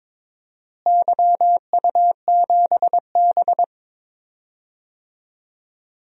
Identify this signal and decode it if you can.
Morse code, heard as a keyed tone, about 21 wpm, 700 Hz. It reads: YU7B
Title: DXCC-100 Call Signs - 4 Characters - Repeated 3x - 22wpm